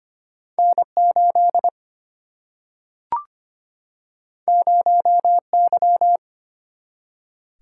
Morse code, heard as a keyed tone, about 25 wpm, 700 Hz. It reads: N 8 0 Y
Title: Single-Letter Single-Number 25wpm